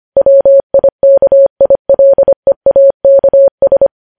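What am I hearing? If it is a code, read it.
Morse code, heard as a keyed tone, about 25 wpm, 550 Hz. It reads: WIKILEAKS
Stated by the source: Morse practice sending news headlines at 25 wpm.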